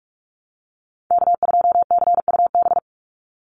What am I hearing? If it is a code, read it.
Morse code, heard as a keyed tone, about 44 wpm, 700 Hz. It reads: K2CUB
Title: Call Signs - 45wpm